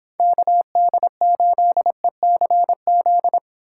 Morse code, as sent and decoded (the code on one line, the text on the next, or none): -.- -.. ---.. -.-. --..
KD8ECZ